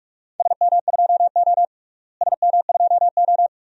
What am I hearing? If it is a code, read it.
Morse code, heard as a keyed tone, about 44 wpm, 700 Hz. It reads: SM2O SM2O
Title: DXCC-100 Call Signs - 4 Characters - Repeated 2x - 45wpm